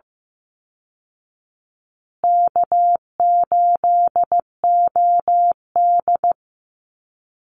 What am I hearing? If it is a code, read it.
Morse code, heard as a keyed tone, about 14 wpm, 700 Hz. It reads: K8OD
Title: Easy Call Signs Repeated 3x - 15wpm